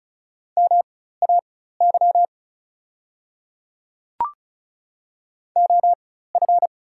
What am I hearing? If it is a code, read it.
Morse code, heard as a keyed tone, about 36 wpm, spaced as for 16 wpm, 690 Hz. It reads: MAY OF
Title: 4x Character Spacing - Top 100 Words - 35wpm